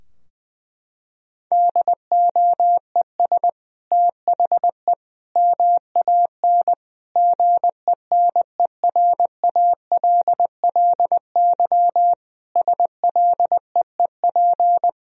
Morse code, heard as a keyed tone, about 20 wpm, 700 Hz. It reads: DOES THE MAN GENERALLY SLEEP
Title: Sentences from Top 1000 Words - Review Part 4 - 20wpm